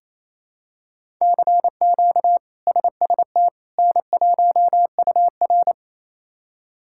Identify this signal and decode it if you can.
Morse code, heard as a keyed tone, about 28 wpm, 700 Hz. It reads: CQ SST N1UR
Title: K1USN SST QSO Practice - 28wpm